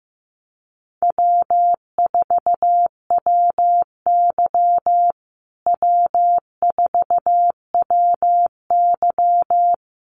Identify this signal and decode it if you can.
Morse code, heard as a keyed tone, about 15 wpm, 700 Hz. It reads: W4WY W4WY
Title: USA Call Signs Repeated 3x 15wpm